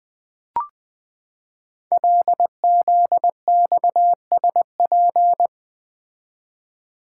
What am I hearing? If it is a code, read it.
Morse code, heard as a keyed tone, about 20 wpm, 700 Hz. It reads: LZXSP
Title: Code Groups - 5 Letters - 20wpm